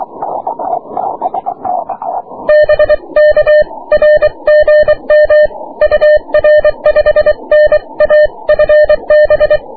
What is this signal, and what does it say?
Morse code, heard as a keyed tone, about 23 wpm, 600 Hz. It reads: BK R GM UR 5NA FB
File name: wp / 6m aurora cw 22jul 2009